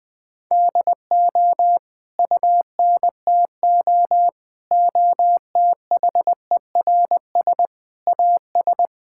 Morse code, heard as a keyed tone, about 20 wpm, 700 Hz. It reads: DO UNTO OTHERS AS
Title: Sentences with 9-Word Phrases - Spoken Beginning - 20wpm